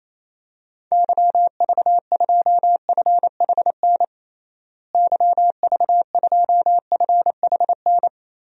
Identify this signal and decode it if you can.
Morse code, heard as a keyed tone, about 28 wpm, 700 Hz. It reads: YV2FHN YV2FHN
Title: DXCC-100 Call Signs - 6 Characters - Repeated 3x - 28wpm